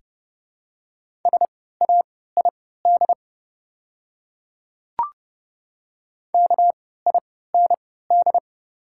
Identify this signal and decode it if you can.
Morse code, heard as a keyed tone, about 30 wpm, spaced as for 17 wpm, 700 Hz. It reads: SAID KIND
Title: 3x Character Spacing - Top 300 Words - 30wpm